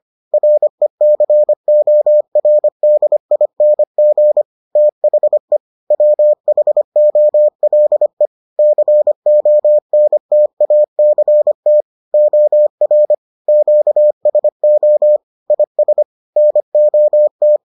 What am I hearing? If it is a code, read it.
Morse code, heard as a keyed tone, about 25 wpm, 600 Hz. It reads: RECORDING THE WHOLE CONTACT OR QSO IS NOT